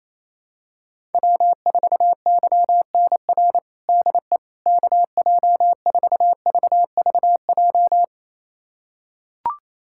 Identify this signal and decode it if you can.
Morse code, heard as a keyed tone, about 28 wpm, 700 Hz. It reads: W4YNR DE KJ4VVJ
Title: USA Callsign DE Callsign 28wpm